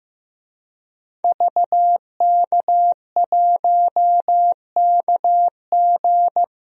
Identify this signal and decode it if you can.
Morse code, heard as a keyed tone, about 15 wpm, 700 Hz. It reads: VK1KG
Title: DXCC-100 Call Signs - 5 Characters - Repeated 2x - 15wpm